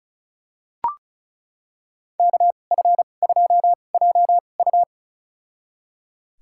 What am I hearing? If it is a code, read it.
Morse code, heard as a keyed tone, about 34 wpm, spaced as for 25 wpm, 700 Hz. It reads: KF2JU
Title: USA Call Signs - 5 Characters - 2x Character Spacing - 35wpm